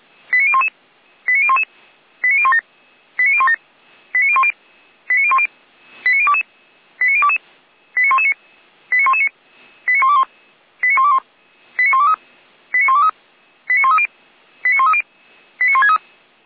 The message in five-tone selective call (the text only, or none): 89011 89011 89017 89017 89010 89010 89022 89022 89119 89119 89121 89121 89123 89123 89130 89130 89163